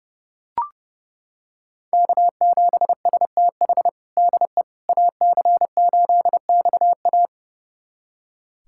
Morse code, heard as a keyed tone, about 30 wpm, 700 Hz. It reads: K7STH DE AC8XA
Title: USA Callsign DE Callsign - 5 Characters - 30wpm